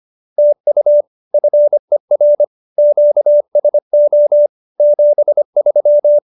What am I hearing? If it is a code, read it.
Morse code, heard as a keyed tone, about 25 wpm, 590 Hz. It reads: TU FER QSO 73